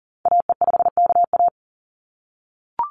Morse code, heard as a keyed tone, about 40 wpm, 700 Hz. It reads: AE5KA
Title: NA SOTA Activator Callsigns 40wpm